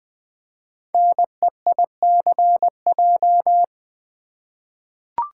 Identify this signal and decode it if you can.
Morse code, heard as a keyed tone, about 20 wpm, 700 Hz. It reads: NEICJ
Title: Code Groups - 5 Letters - 20wpm